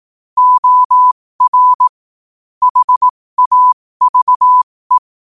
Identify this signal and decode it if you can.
Morse code, heard as a keyed tone, about 18 wpm, spaced as for 15 wpm, 1000 Hz. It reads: OR HAVE